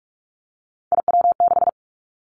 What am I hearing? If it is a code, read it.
Morse code, heard as a keyed tone, about 46 wpm, 700 Hz. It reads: IW6
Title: Three Challenging Letter-Number Combinations 45wpm